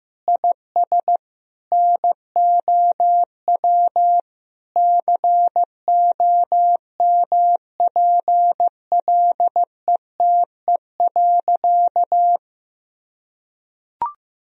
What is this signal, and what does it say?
Morse code, heard as a keyed tone, about 15 wpm, 700 Hz. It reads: IS NOW COMPLETE.
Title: Sentences from Top 1000 Words - Review Part 2 - 15wpm